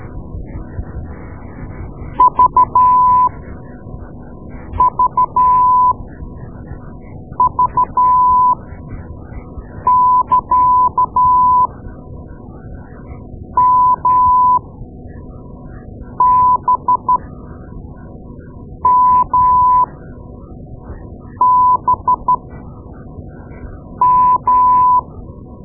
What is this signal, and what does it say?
Morse code, heard as a keyed tone, about 13 wpm, spaced as for 5 wpm, 1000 Hz. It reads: VVV <KA> MBMBM
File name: Morse Lessons / Morse Lessons / Tape-2B